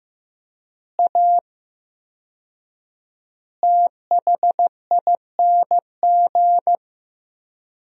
Morse code, heard as a keyed tone, about 15 wpm, 700 Hz. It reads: A THING
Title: 4x Word-Spacing - Sentences from Top 300 Words - 15wpm